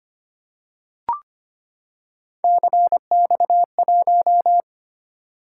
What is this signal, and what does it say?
Morse code, heard as a keyed tone, about 25 wpm, 700 Hz. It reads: CX1
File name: Three Letter-Number Combinations - 25wpm